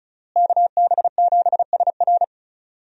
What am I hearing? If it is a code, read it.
Morse code, heard as a keyed tone, about 34 wpm, 700 Hz. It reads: KB7SR
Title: USA Call Signs - 5 Characters - Repeated 3x - 35wpm